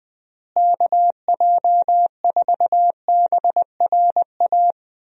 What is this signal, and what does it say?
Morse code, heard as a keyed tone, about 20 wpm, 700 Hz. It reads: KJ4BRA
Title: USA Call Signs - 6 Characters - Repeated 3x - 20wpm